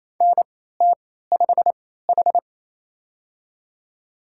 Morse code, heard as a keyed tone, about 28 wpm, spaced as for 16 wpm, 700 Hz. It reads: NT5H
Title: USA Call Signs - 4 Characters - 3x Character Spacing - 28wpm